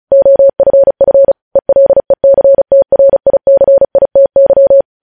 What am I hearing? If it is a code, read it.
Morse code, heard as a keyed tone, about 36 wpm, 550 Hz. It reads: OFF ELECTRICITY
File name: podcast / News headlines-1x-35x35